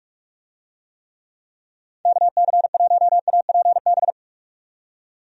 Morse code, heard as a keyed tone, about 43 wpm, 700 Hz. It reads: KC1APB
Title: USA Call Signs - 6 Characters - 45wpm